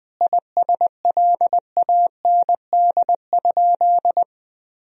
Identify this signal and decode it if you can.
Morse code, heard as a keyed tone, about 21 wpm, 700 Hz. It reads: ISLAND?
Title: Sentences from Top 1000 Words - Review Part 3 - 20wpm